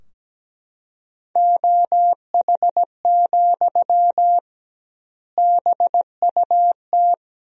Morse code, heard as a keyed tone, about 17 wpm, 700 Hz. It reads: OH, BUT
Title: 2x Word-Spacing - Sentences from Top 500 Words - 17wpm